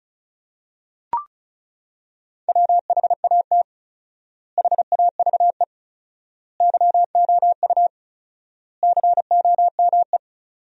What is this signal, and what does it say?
Morse code, heard as a keyed tone, about 36 wpm, 700 Hz. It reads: WHAT HAVE YOU COME
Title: 4x Word-Spacing - Sentences from Top 100 Words - 35wpm